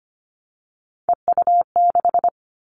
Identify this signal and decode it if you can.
Morse code, heard as a keyed tone, about 25 wpm, 700 Hz. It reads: EU6
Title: Callsign Prefixes for Top 100 DXCC - 25wpm